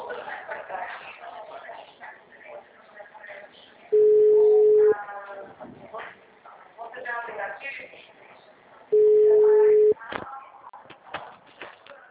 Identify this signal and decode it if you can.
Ringback tone (Europe, 425 Hz), with speech.